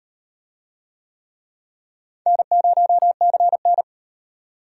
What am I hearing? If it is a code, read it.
Morse code, heard as a keyed tone, about 32 wpm, 700 Hz. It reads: N0CN